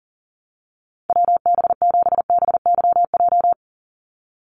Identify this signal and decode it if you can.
Morse code, heard as a keyed tone, about 37 wpm, 700 Hz. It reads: WB7BYJ
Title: USA Call Signs 40wpm